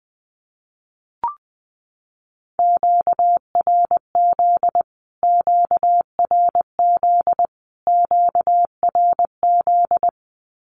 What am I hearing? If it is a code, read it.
Morse code, heard as a keyed tone, about 20 wpm, 700 Hz. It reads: QRZ QRZ QRZ
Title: Q-Codes Repeated 3x 20wpm